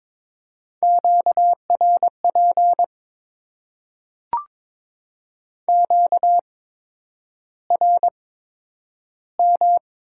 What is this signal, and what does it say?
Morse code, heard as a keyed tone, about 22 wpm, spaced as for 9 wpm, 700 Hz. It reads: QRP QRM
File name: Top 90 QSO Elements - 8x Character Spacing - 22wpm